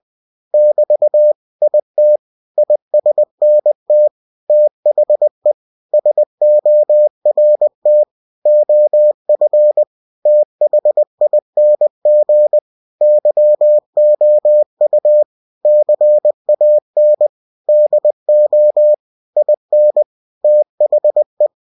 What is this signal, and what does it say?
Morse code, heard as a keyed tone, about 20 wpm, 600 Hz. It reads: <BT> IT ISNT THE SORT OF THING YOU CAN DO IN THE
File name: cw-books / winnie-the-pooh / winnie-20wpm-page-0024